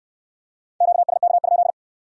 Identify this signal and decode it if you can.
Morse code, heard as a keyed tone, about 65 wpm, 700 Hz. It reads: QSK?